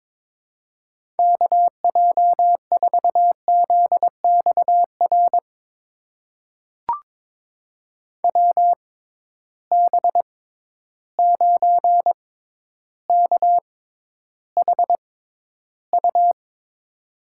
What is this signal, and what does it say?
Morse code, heard as a keyed tone, about 22 wpm, spaced as for 11 wpm, 700 Hz. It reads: KJ4ZXR WB9KHU
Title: USA Call Signs - 6 Characters - 6x Character Spacing - 22wpm